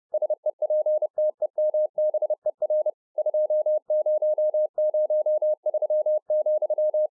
Morse code, heard as a keyed tone, about 30 wpm, 610 Hz. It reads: SEPTEMBER 2003,